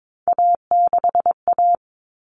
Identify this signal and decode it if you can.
Morse code, heard as a keyed tone, about 22 wpm, 700 Hz. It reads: A6A
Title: Common Cut Numbers - 3-Digit 22wpm